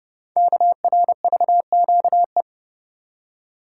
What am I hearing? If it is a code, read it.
Morse code, heard as a keyed tone, about 30 wpm, 690 Hz. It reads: KRVQE
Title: Code Groups - 5 Letters - 30wpm